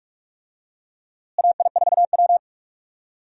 Morse code, heard as a keyed tone, about 45 wpm, 700 Hz. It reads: AI4W
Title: USA Call Signs - 4 Characters - Repeated 2x - 45wpm